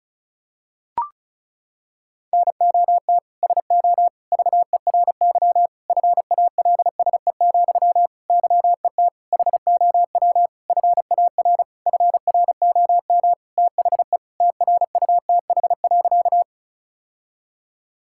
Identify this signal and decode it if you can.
Morse code, heard as a keyed tone, about 34 wpm, 700 Hz. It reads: NOT SO VERY FALSE, YET HOW FAR FROM THE TRUTH.
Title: Sentences from Top 1600 Words 35wpm